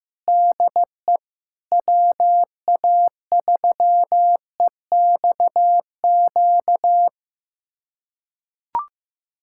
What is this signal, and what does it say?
Morse code, heard as a keyed tone, about 16 wpm, 700 Hz. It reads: DE WA3EXQ
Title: USA Callsign DE Callsign - 6 Characters - 15wpm